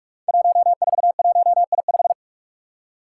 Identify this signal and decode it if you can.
Morse code, heard as a keyed tone, about 45 wpm, 700 Hz. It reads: 141I5
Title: Five Challenging Letter-Number Combinations 45wpm